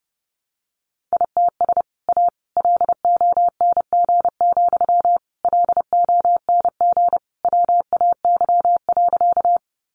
Morse code, heard as a keyed tone, about 30 wpm, 700 Hz. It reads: ITS A LONG, LONG WAY.
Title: Sentences from Top 600 Words 30wpm